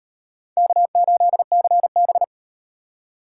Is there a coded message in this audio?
Yes: Morse code from 0.6 to 2.2 s, heard as a keyed tone, about 35 wpm, 700 Hz. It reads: K8CB